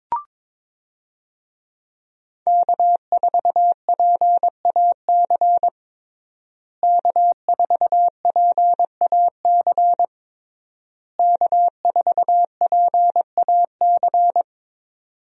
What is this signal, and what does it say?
Morse code, heard as a keyed tone, about 22 wpm, 700 Hz. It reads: K4PAC K4PAC K4PAC
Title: USA Call Signs - 5 Characters - Repeated 3x - 22wpm